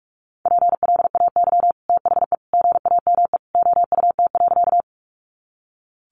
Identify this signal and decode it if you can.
Morse code, heard as a keyed tone, about 44 wpm, 700 Hz. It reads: PLAY THE GAME OUT.
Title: Sentences from Top 500 Words 45wpm